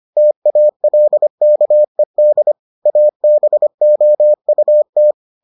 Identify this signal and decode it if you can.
Morse code, heard as a keyed tone, about 25 wpm, 600 Hz. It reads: TALKED ABOUT